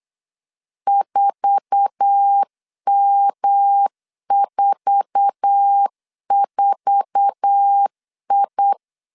Morse code, heard as a keyed tone, about 8 wpm, 800 Hz. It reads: 4M44I
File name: LEZIONI / 8 YJ45 / 8 YJ4 8wpm 3246